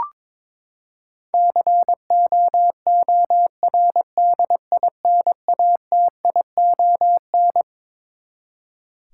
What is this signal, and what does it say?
Morse code, heard as a keyed tone, about 22 wpm, 700 Hz. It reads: COORDINATION
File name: Twelve Letter Words - 22wpm